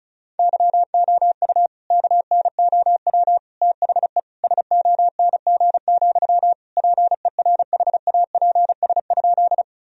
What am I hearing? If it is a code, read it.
Morse code, heard as a keyed tone, about 34 wpm, 700 Hz. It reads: YOU KNOW THE SONG, PERHAPS?
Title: Sentences from Top 1000 Words - Review Part 2 - 35wpm